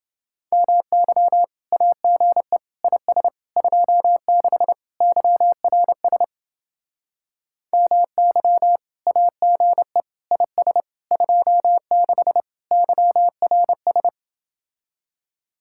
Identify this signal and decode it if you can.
Morse code, heard as a keyed tone, about 29 wpm, 700 Hz. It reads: MY AGE IS 26 YRS MY AGE IS 26 YRS